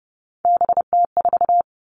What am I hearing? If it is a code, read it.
Morse code, heard as a keyed tone, about 30 wpm, 700 Hz. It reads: BT4
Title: All Cut Numbers - 3-Digit 30wpm